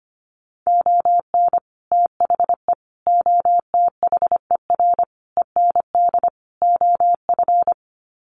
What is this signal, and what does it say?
Morse code, heard as a keyed tone, about 25 wpm, 700 Hz. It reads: ON THE OTHER END OF
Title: Sets of 5 Words Encore 25wpm